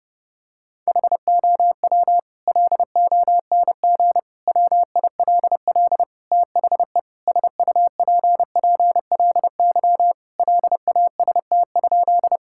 Morse code, heard as a keyed tone, about 30 wpm, 700 Hz. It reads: HOW LONG WILL THE SUPPLY LAST?
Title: Sentences from Top 1000 Words - Review Part 5 - 30wpm